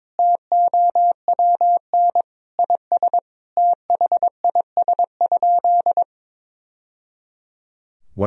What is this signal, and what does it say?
Morse code, heard as a keyed tone, about 22 wpm, 700 Hz. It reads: TOWN IS THIS?
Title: Sentences from Top 1000 Words - Review Part 3 - 22wpm